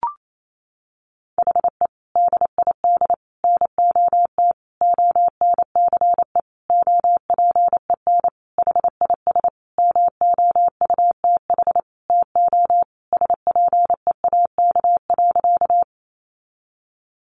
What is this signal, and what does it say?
Morse code, heard as a keyed tone, about 28 wpm, 700 Hz. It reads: HE DID NOT ONCE OPEN HIS MOUTH TO SPEAK.
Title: Sentences from Top 1500 Words 28wpm